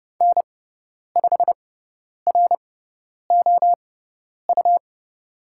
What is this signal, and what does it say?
Morse code, heard as a keyed tone, about 30 wpm, spaced as for 10 wpm, 700 Hz. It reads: N5ROU
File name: Lesson 17g - Letter u - Call Signs 30wpm